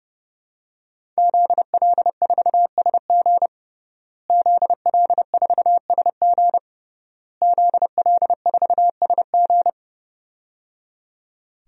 Morse code, heard as a keyed tone, about 30 wpm, 700 Hz. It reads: ZL4SG ZL4SG ZL4SG
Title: DXCC-100 Call Signs - 5 Characters - Repeated 3x - 30wpm